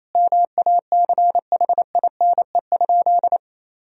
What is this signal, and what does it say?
Morse code, heard as a keyed tone, about 28 wpm, 700 Hz. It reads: MACHINE?
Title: Sentences from Top 1000 Words - Review Part 2 - 28wpm